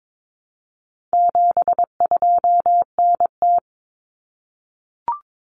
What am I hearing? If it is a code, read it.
Morse code, heard as a keyed tone, about 22 wpm, 700 Hz. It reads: 72NT
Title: Common Cut Numbers - 4-Digit 22wpm